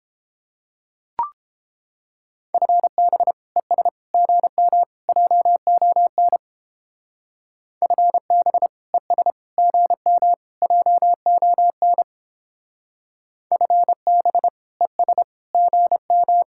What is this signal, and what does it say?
Morse code, heard as a keyed tone, about 30 wpm, 700 Hz. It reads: FB ES GM JON FB ES GM JON FB ES GM